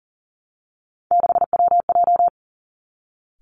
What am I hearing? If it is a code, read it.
Morse code, heard as a keyed tone, about 42 wpm, 690 Hz. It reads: 6WJ
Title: Three Challenging Letter-Number Combinations 40wpm